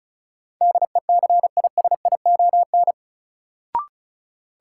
Morse code, heard as a keyed tone, about 36 wpm, 700 Hz. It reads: DECISION
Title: Sentences from Top 1000 Words - New Word Drill - 35wpm